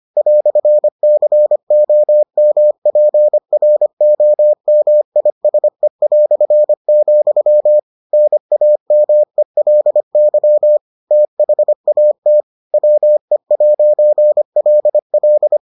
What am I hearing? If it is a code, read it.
Morse code, heard as a keyed tone, about 25 wpm, 600 Hz. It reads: "COMPROMISE", NAMELY THAT WE'LL